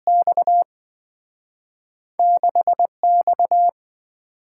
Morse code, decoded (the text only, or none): X 6X